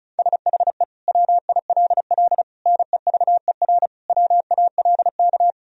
Morse code, heard as a keyed tone, about 36 wpm, 700 Hz. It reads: SHE WILL NEVER WALK
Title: Sentences from Top 1000 Words - Review Part 1 - 35wpm